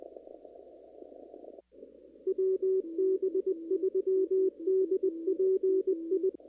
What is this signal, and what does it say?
Morse code, heard as a keyed tone, about 20 wpm, 380 Hz, with radio noise. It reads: WB3DPI
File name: CW / kt5j wpx cw-op-k5pi